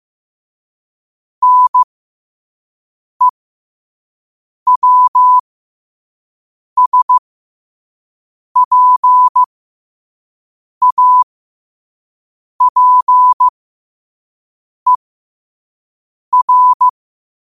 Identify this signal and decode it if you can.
Morse code, heard as a keyed tone, about 15 wpm, spaced as for 5 wpm, 1000 Hz. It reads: NEWSPAPER
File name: qotdfiles / 5wpm